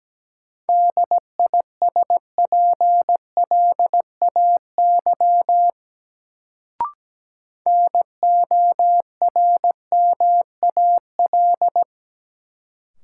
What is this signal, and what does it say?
Morse code, heard as a keyed tone, about 17 wpm, 700 Hz. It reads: DISPLAY NORMAL
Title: Sentences from Top 1200 Words - New Word Drill - 17wpm